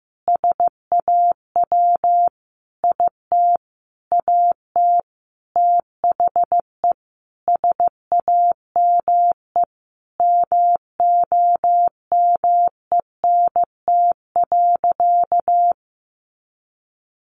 Morse code, heard as a keyed tone, about 15 wpm, 700 Hz. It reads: SAW IT AT THE SAME MOMENT.
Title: Sentences from Top 1000 Words 15wpm